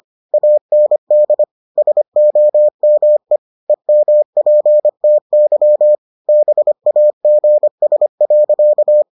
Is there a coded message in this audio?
Yes: Morse code from 0.3 to 9.0 s, heard as a keyed tone, about 25 wpm, 600 Hz. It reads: AND SOME EMPTY BAGS.